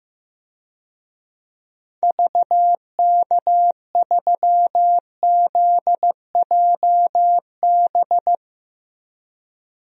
Morse code, heard as a keyed tone, about 15 wpm, 700 Hz. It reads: VK3ZJB
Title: DXCC-100 Call Signs - 6 Characters - Repeated 2x - 15wpm